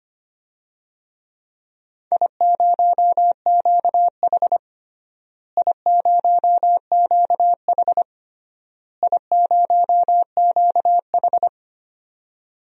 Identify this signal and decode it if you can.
Morse code, heard as a keyed tone, about 25 wpm, 700 Hz. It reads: I0QH I0QH I0QH
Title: DXCC-100 Call Signs - 4 Characters - Repeated 3x - 25wpm